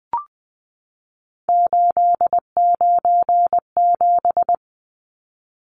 Morse code, heard as a keyed tone, about 20 wpm, 700 Hz. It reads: 897
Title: Numbers - 3-Digit 20wpm